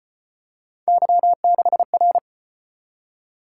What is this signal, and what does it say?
Morse code, heard as a keyed tone, about 33 wpm, 700 Hz. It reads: Y6R